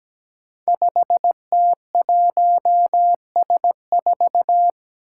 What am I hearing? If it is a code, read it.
Morse code, heard as a keyed tone, about 17 wpm, 700 Hz. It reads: 5T1S4
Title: Five Challenging Letter-Number Combinations 17wpm